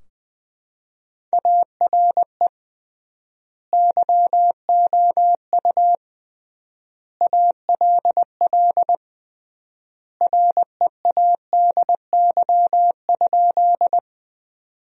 Morse code, heard as a keyed tone, about 20 wpm, 700 Hz. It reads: ARE YOU ALL READY?
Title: 3x Word-Spacing - Sentences from Top 400 Words - 20wpm